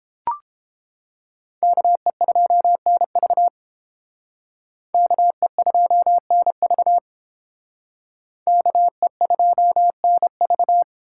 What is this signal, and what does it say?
Morse code, heard as a keyed tone, about 30 wpm, 700 Hz. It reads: KE2NV KE2NV KE2NV